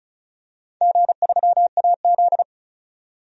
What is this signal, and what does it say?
Morse code, heard as a keyed tone, about 36 wpm, 700 Hz. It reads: G3AZ
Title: DXCC-100 Call Signs - 4 Characters - 35wpm